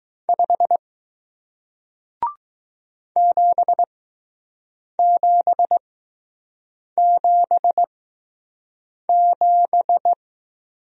Morse code, heard as a keyed tone, about 20 wpm, spaced as for 6 wpm, 700 Hz. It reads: 5 7777